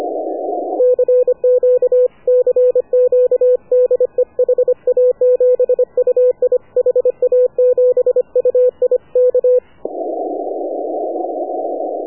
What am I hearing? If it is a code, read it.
Morse code, heard as a keyed tone, about 25 wpm, 500 Hz. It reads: CQ CQ DE HA7UI HA7UI K
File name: AUDIO2019 / HA7UI CW